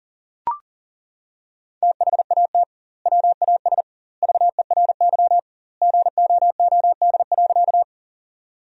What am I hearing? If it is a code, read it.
Morse code, heard as a keyed tone, about 40 wpm, 700 Hz. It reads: THAT WAS VERY GOOD.
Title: 2x Word-Spacing - Sentences from Top 200 Words - 40wpm